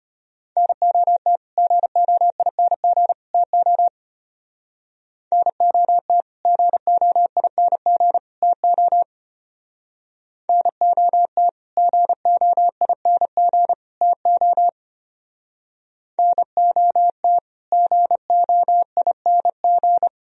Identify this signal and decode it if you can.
Morse code, heard as a keyed tone, about 31 wpm, 700 Hz. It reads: NOT GOING TO NOT GOING TO NOT GOING TO NOT GOING
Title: Speed-Racing - Sets of 3 Words 25wpm